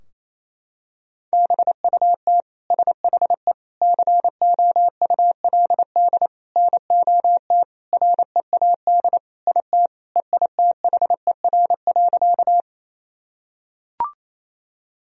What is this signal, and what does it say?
Morse code, heard as a keyed tone, about 28 wpm, 700 Hz. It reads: BUT SHE COULD NOT READ IT EITHER.
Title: Sentences from Top 1000 Words - Review Part 4 - 28wpm